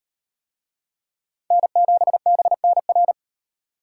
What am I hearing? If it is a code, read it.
Morse code, heard as a keyed tone, about 38 wpm, 700 Hz. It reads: N7BNR